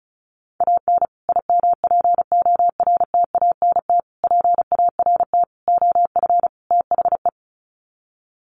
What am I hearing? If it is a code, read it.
Morse code, heard as a keyed tone, about 34 wpm, 700 Hz. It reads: AN IMPORTANT PART OF THE
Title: Sets of 5 Words Encore 35wpm